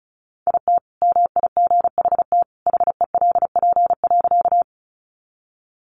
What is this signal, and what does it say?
Morse code, heard as a keyed tone, about 34 wpm, 700 Hz. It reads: IT MIGHT HELP.
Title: Sentences from Top 300 Words 35wpm